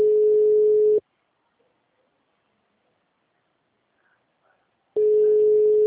Ringback tone (Europe, 425 Hz).